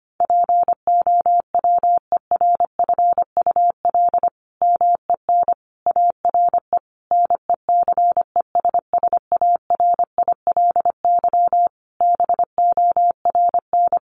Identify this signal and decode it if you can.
Morse code, heard as a keyed tone, about 25 wpm, 700 Hz. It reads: POWERFUL MEN ARE NECESSARILY BORN
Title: Sentences from Top 1400 Words 25wpm